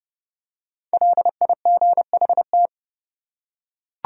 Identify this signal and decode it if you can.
Morse code, heard as a keyed tone, about 29 wpm, 700 Hz. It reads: LIGHT